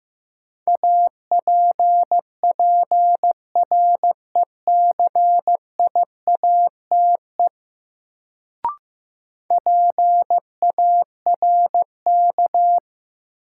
Morse code, heard as a keyed tone, about 15 wpm, 700 Hz. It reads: APPRECIATE PARK
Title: Sentences from Top 1300 Words - New Word Drill Alt - 15wpm